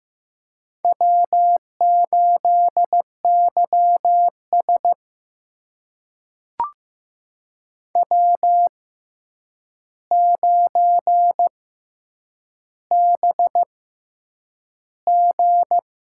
Morse code, heard as a keyed tone, about 15 wpm, spaced as for 8 wpm, 700 Hz. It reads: W8YS W9BG
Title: USA Call Signs - 4 Characters - 6x Character Spacing - 15wpm